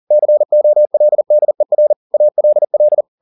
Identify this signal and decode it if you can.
Morse code, heard as a keyed tone, about 40 wpm, 600 Hz. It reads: COLDER ALL
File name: the old man and the sea 2 40wpm